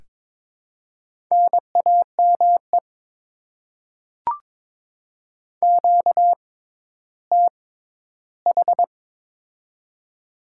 Morse code, heard as a keyed tone, about 22 wpm, spaced as for 13 wpm, 700 Hz. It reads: NAME QTH
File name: Top 20 QSO Elements - 6x Character Spacing - 22wpm